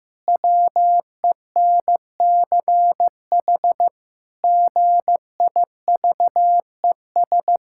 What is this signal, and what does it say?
Morse code, heard as a keyed tone, about 16 wpm, 700 Hz. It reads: WENCH GIVES
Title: Pangram Challenge v2 - 15wpm